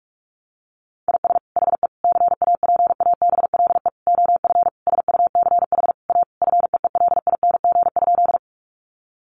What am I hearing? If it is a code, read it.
Morse code, heard as a keyed tone, about 46 wpm, 700 Hz. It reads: IS HE CAPABLE OF SUCH A FEELING?
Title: Sentences from Top 1200 Words 45wpm